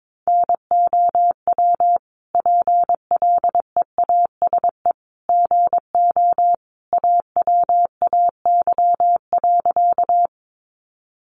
Morse code, heard as a keyed tone, about 22 wpm, 700 Hz. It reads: NOW PLEASE GO AWAY.
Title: Sentences from Top 1500 Words 22wpm